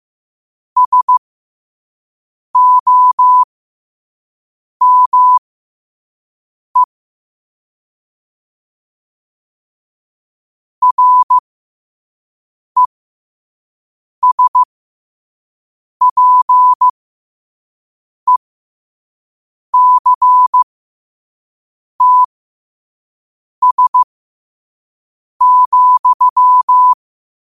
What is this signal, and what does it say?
Morse code, heard as a keyed tone, about 15 wpm, spaced as for 5 wpm, 1000 Hz. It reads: SOME RESPECTS,
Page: Quotes for Wed, 13 Aug 2025 in Morse Code at 5 words per minute.